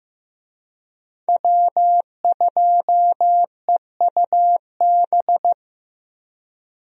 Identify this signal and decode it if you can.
Morse code, heard as a keyed tone, about 15 wpm, 700 Hz. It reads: W2EUB
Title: USA Call Signs - 5 Characters - Repeated 2x - 15wpm